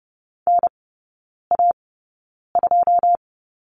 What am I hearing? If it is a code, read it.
Morse code, heard as a keyed tone, about 30 wpm, spaced as for 9 wpm, 700 Hz. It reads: NA2
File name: 081 Lesson 14e - Number 2 - Three Characters 30wpm